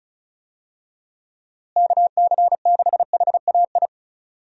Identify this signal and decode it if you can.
Morse code, heard as a keyed tone, about 36 wpm, 700 Hz. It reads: KC6HAI
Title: USA Call Signs - 6 Characters - Repeated 3x - 35wpm